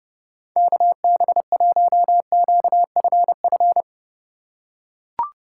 Morse code, heard as a keyed tone, about 30 wpm, 700 Hz. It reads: KB1QFF
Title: USA Call Signs - 6 Characters - 3x Character Spacing - 30wpm